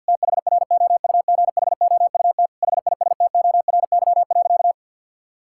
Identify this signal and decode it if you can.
Morse code, heard as a keyed tone, about 50 wpm, 690 Hz. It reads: THROUGHOUT HISTORY.
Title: Sentences from Top 1000 Words - Review Part 5 - 50wpm